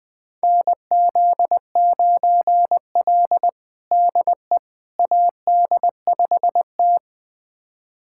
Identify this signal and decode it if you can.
Morse code, heard as a keyed tone, about 21 wpm, 700 Hz. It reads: NZ9L DE AD5T
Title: USA Callsign DE Callsign - 4 Characters - 20wpm